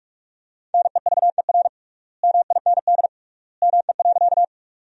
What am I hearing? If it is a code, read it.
Morse code, heard as a keyed tone, about 46 wpm, 700 Hz. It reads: NEVER MIND ME.
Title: 3x Word-Spacing - Sentences from Top 500 Words - 45wpm